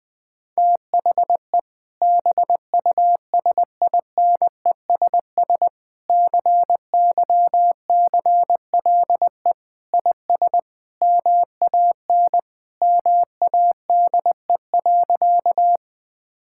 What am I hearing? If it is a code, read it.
Morse code, heard as a keyed tone, about 21 wpm, 700 Hz. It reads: THE BUSINESS CYCLE IS MAN MADE.
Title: Sentences from Top 1000 Words - Review Part 6 - 20wpm